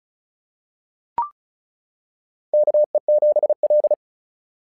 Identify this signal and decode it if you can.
Morse code, heard as a keyed tone, about 34 wpm, 660 Hz. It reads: KE7L
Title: SOTA QSO Practice 35wpm